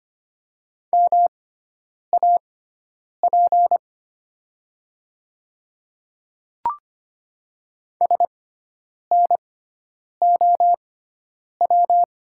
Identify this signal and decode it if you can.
Morse code, heard as a keyed tone, about 25 wpm, spaced as for 9 wpm, 700 Hz. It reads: MAP SNOW
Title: 6x Character Spacing - Top 500 Words - 25wpm